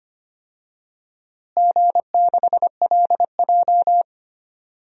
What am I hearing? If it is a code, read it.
Morse code, heard as a keyed tone, about 25 wpm, 700 Hz. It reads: G6LJ
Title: DXCC-100 Call Signs - 4 Characters - Repeated 3x - 25wpm